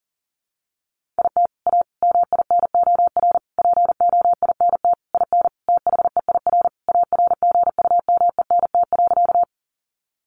Morse code, heard as a keyed tone, about 40 wpm, 700 Hz. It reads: IT A MINOR POINT IN THEIR ARGUMENT.
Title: Sentences from Top 1500 Words 40wpm